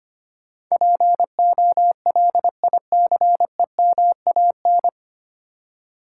Morse code, heard as a keyed tone, about 25 wpm, 700 Hz. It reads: POLICEMAN
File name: practice / 25wpm / 9 Letter Words - First 3 Letters Given - 25wpm